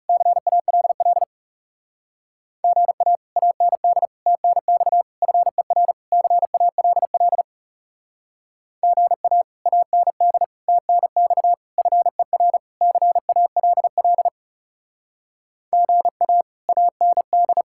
Morse code, heard as a keyed tone, about 36 wpm, 700 Hz. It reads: CALL GA AND TNX FER CALL GA AND TNX FER CALL GA AND
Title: Speed-Racing Common QSO Phrases - 30wpm